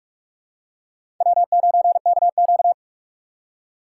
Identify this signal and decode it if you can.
Morse code, heard as a keyed tone, about 46 wpm, 700 Hz. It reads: W9KQ
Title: USA Call Signs - 2x Character Spacing - 45wpm